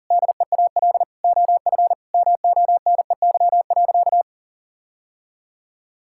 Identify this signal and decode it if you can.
Morse code, heard as a keyed tone, about 40 wpm, 700 Hz. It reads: DEAL OF MONEY.
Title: Sentences from Top 1000 Words - Review Part 6 - 40wpm